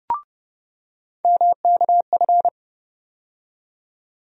Morse code, heard as a keyed tone, about 30 wpm, 700 Hz. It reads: MKF
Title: USA Call Sign Postfixes - 3 Letters 30wpm